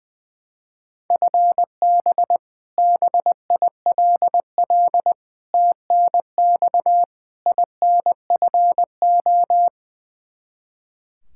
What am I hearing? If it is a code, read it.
Morse code, heard as a keyed tone, about 20 wpm, 700 Hz. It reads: FB BILL TNX INFO
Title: Practice QSO - Exchange 3 - More Info - 20wpm